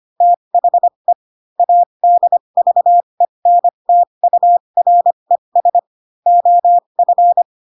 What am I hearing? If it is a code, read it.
Morse code, heard as a keyed tone, about 25 wpm, spaced as for 22 wpm, 700 Hz. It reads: THE ADVENTURES OF
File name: Pinocchio / Pinocchio ch6 18 25 ik1hge